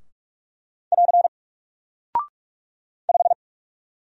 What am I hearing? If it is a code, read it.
Morse code, heard as a keyed tone, about 44 wpm, 700 Hz. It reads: <AR> 5
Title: Single Character (Letter:Number:Punctuation:Prosign) Alt 45wpm